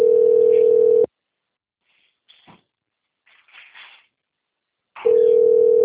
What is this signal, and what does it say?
Ringback tone (North America), 440 + 480 Hz.